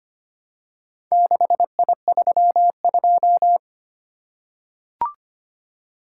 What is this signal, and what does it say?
Morse code, heard as a keyed tone, about 25 wpm, 700 Hz. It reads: 6I32
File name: practice / 25wpm / Four Letter-Number Combinations - 25wpm